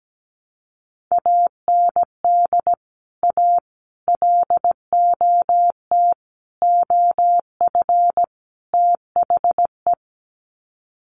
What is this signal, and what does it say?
Morse code, heard as a keyed tone, about 17 wpm, 700 Hz. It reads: AND A LOT OF THE
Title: Sets of 5 Words Encore 17wpm